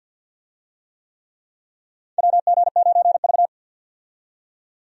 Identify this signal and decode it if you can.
Morse code, heard as a keyed tone, about 50 wpm, 700 Hz. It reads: WG9V
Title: USA Call Signs - 4 Characters - 50wpm